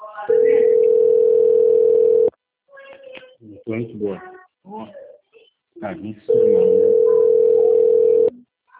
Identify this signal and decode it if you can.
Ringback tone (North America), 440 + 480 Hz, with speech.
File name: out- - - - - .4604